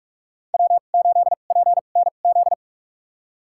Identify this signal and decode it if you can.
Morse code, heard as a keyed tone, about 43 wpm, spaced as for 32 wpm, 700 Hz. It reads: W8PNZ
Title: USA Call Signs - 2x Character Spacing - 45wpm